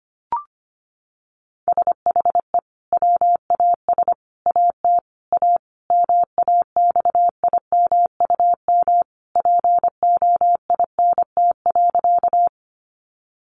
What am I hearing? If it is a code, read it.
Morse code, heard as a keyed tone, about 25 wpm, 700 Hz. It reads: SHE WAS AT A MAXIMUM POINT.
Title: Sentences from Top 1300 Words 25wpm